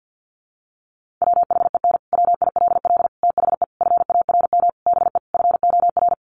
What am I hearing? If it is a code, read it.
Morse code, heard as a keyed tone, about 48 wpm, 700 Hz. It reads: WHEN WILL THE FARM BE FOR
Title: Sentences from Top 1400 Words 50wpm